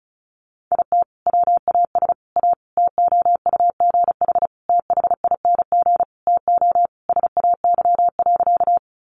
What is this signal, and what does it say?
Morse code, heard as a keyed tone, about 34 wpm, 700 Hz. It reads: IT WAS A TOUGH THING TO SAY.
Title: Sentences from Top 1200 Words 35wpm